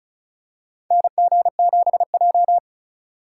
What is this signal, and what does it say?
Morse code, heard as a keyed tone, about 36 wpm, 690 Hz. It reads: NG7J
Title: USA Call Signs - 4 Characters - 2x Character Spacing - 35wpm